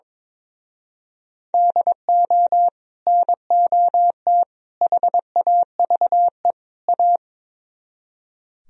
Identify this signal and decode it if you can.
Morse code, heard as a keyed tone, about 22 wpm, 700 Hz. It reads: DO NOT HAVE A
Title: Speed-Racing - Sets of 4 Words 22wpm